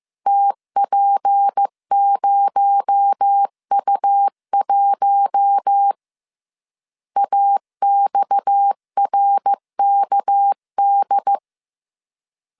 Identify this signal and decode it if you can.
Morse code, heard as a keyed tone, about 15 wpm, 800 Hz. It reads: TP0U1 AXRKD
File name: CW / DET / 1 6 1500C S B 14wpm 2958